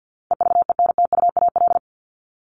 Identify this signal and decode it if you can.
Morse code, heard as a keyed tone, about 50 wpm, 700 Hz. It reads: EVENTUAL
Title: LY Suffix - Top 100 Words with Base Words 50wpm